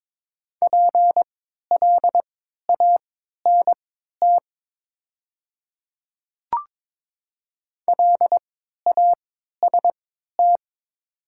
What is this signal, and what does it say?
Morse code, heard as a keyed tone, about 22 wpm, spaced as for 13 wpm, 700 Hz. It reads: PLANT LAST
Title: 3x Character Spacing - Top 500 Words - 22wpm